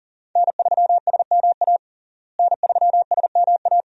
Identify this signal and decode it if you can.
Morse code, heard as a keyed tone, about 40 wpm, 700 Hz. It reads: N3SMA N3SMA
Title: USA Call Signs - 5 Characters - Repeated 2x - 40wpm